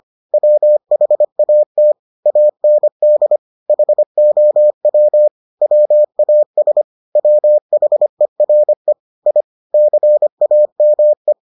Morse code, heard as a keyed tone, about 25 wpm, 600 Hz. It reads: WHAT AND HOW WAS WHERE I CAME